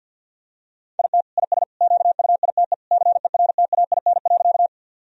Morse code, heard as a keyed tone, about 51 wpm, 700 Hz. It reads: IT IS QUITE CERTAIN.
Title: Sentences from Top 1000 Words - Review Part 2 - 50wpm